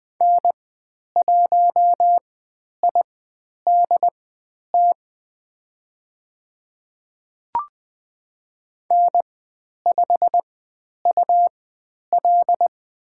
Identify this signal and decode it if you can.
Morse code, heard as a keyed tone, about 20 wpm, spaced as for 10 wpm, 700 Hz. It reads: N1IDT N5UL
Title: 102 Lesson 17g - Letter u - Call Signs 20wpm